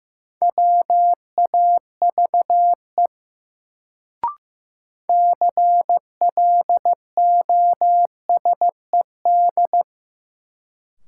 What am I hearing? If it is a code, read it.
Morse code, heard as a keyed tone, about 15 wpm, 700 Hz. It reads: WAVE CLOSED
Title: Sentences from Top 1400 Words - New Word Drill Alt - 15wpm